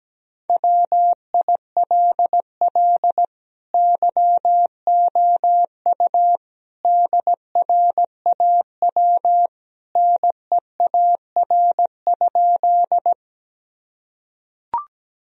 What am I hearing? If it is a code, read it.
Morse code, heard as a keyed tone, about 17 wpm, 700 Hz. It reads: WILL YOU DRAW NEAR?
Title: Sentences from Top 1000 Words - Review Part 1 - 17wpm